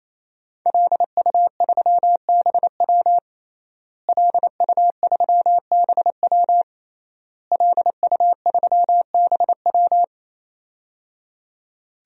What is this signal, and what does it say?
Morse code, heard as a keyed tone, about 28 wpm, 700 Hz. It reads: LU3BW LU3BW LU3BW
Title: DXCC-100 Call Signs - 5 Characters - Repeated 3x - 28wpm